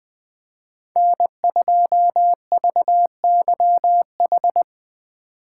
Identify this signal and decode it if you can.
Morse code, heard as a keyed tone, about 20 wpm, 700 Hz. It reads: N2VYH